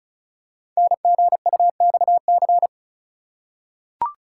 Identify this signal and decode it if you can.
Morse code, heard as a keyed tone, about 36 wpm, 700 Hz. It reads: NGUXC
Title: Code Groups - 5 Letter:Numbers - 35wpm